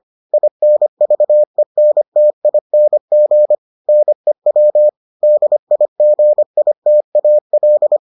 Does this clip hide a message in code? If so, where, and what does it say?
Yes: Morse code from 0.3 to 8.0 s, heard as a keyed tone, about 25 wpm, 600 Hz. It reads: INVENTING NEW DIGITAL